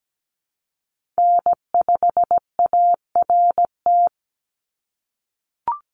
Morse code, heard as a keyed tone, about 17 wpm, 700 Hz. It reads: N5ART
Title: USA Call Signs 17wpm